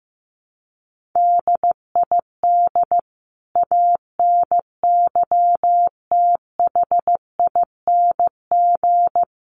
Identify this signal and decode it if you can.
Morse code, heard as a keyed tone, about 14 wpm, 700 Hz. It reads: DID ANYTHING
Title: Sentences from Top 1400 Words 15wpm